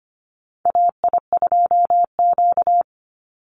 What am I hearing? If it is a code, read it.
Morse code, heard as a keyed tone, about 25 wpm, 700 Hz. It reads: AI2Q
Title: Call Signs Repeated 3x - 25wpm